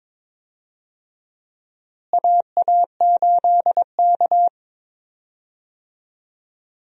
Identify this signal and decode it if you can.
Morse code, heard as a keyed tone, about 22 wpm, 700 Hz. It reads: AA8K
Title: USA Call Signs - 4 Characters - Repeated 3x - 22wpm